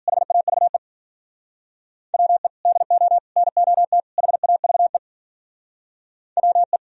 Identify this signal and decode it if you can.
Morse code, heard as a keyed tone, about 52 wpm, 700 Hz. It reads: HAVE WE DO NOT HAVE WE